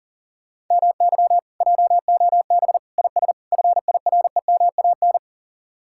Morse code, heard as a keyed tone, about 37 wpm, 700 Hz. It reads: MY JOB IS FIREMAN
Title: Speed-Racing Common QSO Phrases - 30wpm